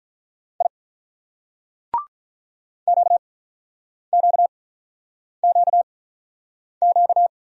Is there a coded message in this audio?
Yes: Morse code from 0.6 to 7.3 s, heard as a keyed tone, about 47 wpm, spaced as for 9 wpm, 700 Hz. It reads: I QQQQ